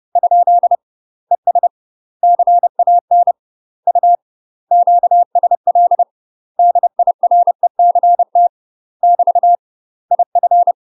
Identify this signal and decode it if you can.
Morse code, heard as a keyed tone, about 30 wpm, 700 Hz. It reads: ? ES CAN U QSL DIRECT <BT> IF
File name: UK MASTER QSOs TRAINING / 30WPM / UK MASTER QSOs 0008